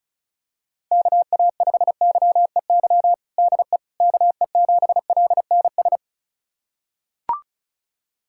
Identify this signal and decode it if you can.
Morse code, heard as a keyed tone, about 36 wpm, 700 Hz. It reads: KA5YEY DE KE7LNS
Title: USA Callsign DE Callsign 35wpm